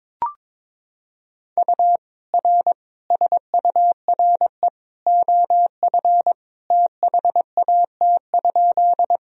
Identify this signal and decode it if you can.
Morse code, heard as a keyed tone, about 22 wpm, 700 Hz. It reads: U R SURE OF THAT?
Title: Sentences from Top 300 Words and Abbreviations 22wpm